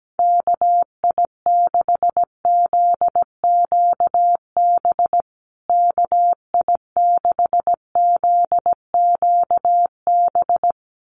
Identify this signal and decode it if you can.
Morse code, heard as a keyed tone, about 17 wpm, 700 Hz. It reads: KI6ZQB KI6ZQB
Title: USA Call Signs Repeated 2x 17wpm